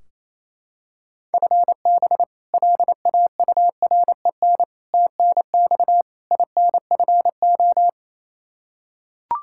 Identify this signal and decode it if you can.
Morse code, heard as a keyed tone, about 28 wpm, 700 Hz. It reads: FB LAUREN TNX INFO
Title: K1USN SST QSO Practice 28wpm